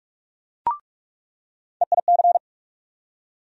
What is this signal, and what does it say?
Morse code, heard as a keyed tone, about 44 wpm, 700 Hz. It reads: EIC
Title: USA Call Sign Postfixes - 3 Letters 45wpm